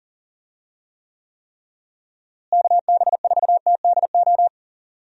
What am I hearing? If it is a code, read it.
Morse code, heard as a keyed tone, about 40 wpm, 700 Hz. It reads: KB4TDO
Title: USA Call Signs - 6 Characters - Repeated 3x - 40wpm